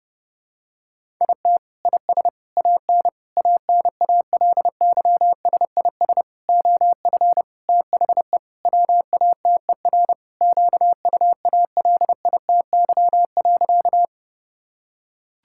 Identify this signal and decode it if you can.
Morse code, heard as a keyed tone, about 30 wpm, 700 Hz. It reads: IT IS AN ANALYSIS OF THE WATER QUALITY.
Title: Sentences from Top 1000 Words - Review Part 5 - 30wpm